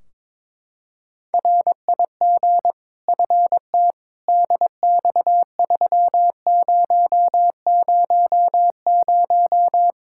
Morse code, heard as a keyed tone, about 22 wpm, 700 Hz. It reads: RIG FT DX3000
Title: K1USN SST QSO Practice 22wpm